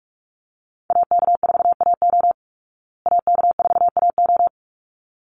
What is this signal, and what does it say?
Morse code, heard as a keyed tone, about 44 wpm, 700 Hz. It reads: AK4AO AK4AO
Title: Call Signs Repeated 3x - 45wpm